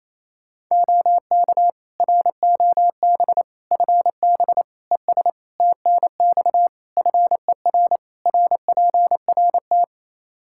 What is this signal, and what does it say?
Morse code, heard as a keyed tone, about 28 wpm, 700 Hz. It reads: OK ROB FB ES TNX FER RPRT
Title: Practice QSO - Exchange 2 - Info - 28wpm